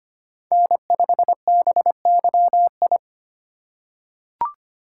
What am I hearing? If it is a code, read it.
Morse code, heard as a keyed tone, about 25 wpm, 700 Hz. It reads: N5BYI
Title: USA Call Signs - 5 Characters - 4x Character Spacing - 25wpm